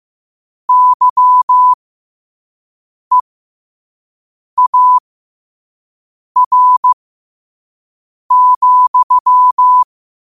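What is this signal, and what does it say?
Morse code, heard as a keyed tone, about 15 wpm, spaced as for 5 wpm, 1000 Hz. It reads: YEAR,
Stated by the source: Morse code sent at 5 wpm.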